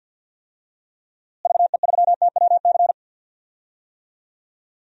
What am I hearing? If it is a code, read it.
Morse code, heard as a keyed tone, about 48 wpm, 700 Hz. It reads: VE3TWC
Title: DXCC-100 Call Signs - 6 Characters - Repeated 3x - 50wpm